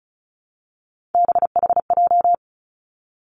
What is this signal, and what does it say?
Morse code, heard as a keyed tone, about 34 wpm, 700 Hz. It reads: BHJ
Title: Three Challenging Letter-Number Combinations 35wpm